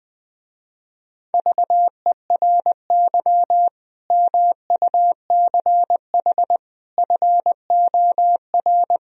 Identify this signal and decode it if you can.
Morse code, heard as a keyed tone, about 20 wpm, 700 Hz. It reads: VERY MUCH FOR
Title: Speed-Racing - Sets of 5 Words 20wpm